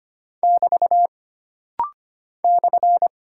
Morse code, heard as a keyed tone, about 25 wpm, 700 Hz. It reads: <BT> /
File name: Single Punctuation-Prosign - Rapid-Fire - 25wpm